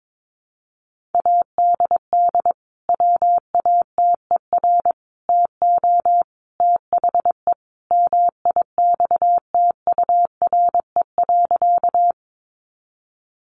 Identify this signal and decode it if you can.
Morse code, heard as a keyed tone, about 22 wpm, 700 Hz. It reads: ADD WATER TO THE MIXTURE.
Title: Sentences from Top 1500 Words 22wpm